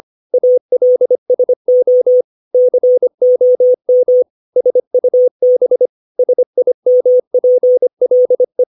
Morse code, heard as a keyed tone, about 25 wpm, 500 Hz. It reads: ALSO COM SUB SIMPLE
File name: 632 words / 632 Words 25WPM